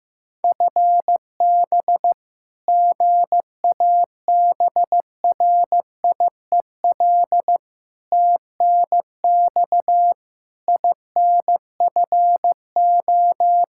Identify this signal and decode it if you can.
Morse code, heard as a keyed tone, about 15 wpm, 700 Hz. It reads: FB GABRIEL TNX INFO
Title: Practice QSO - Exchange 3 - More Info - 15wpm